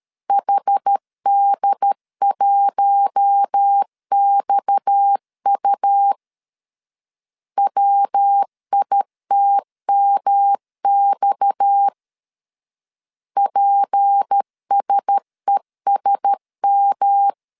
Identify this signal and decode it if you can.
Morse code, heard as a keyed tone, about 13 wpm, 800 Hz. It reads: HD1XU WITMX PSESM
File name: LEZIONI / 6 PKX0 / 6 PKX 12wpm 2024